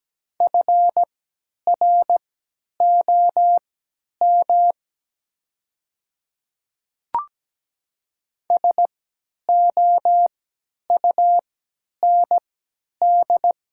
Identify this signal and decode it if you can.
Morse code, heard as a keyed tone, about 17 wpm, spaced as for 10 wpm, 700 Hz. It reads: FROM SOUND
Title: 3x Character Spacing - Top 100 Words - 17wpm